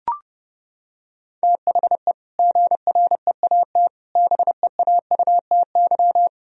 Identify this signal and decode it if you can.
Morse code, heard as a keyed tone, about 30 wpm, 700 Hz. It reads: THE GREAT BEAUTY
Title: Sentences from Top 1000 Words - Review Part 2 - 30wpm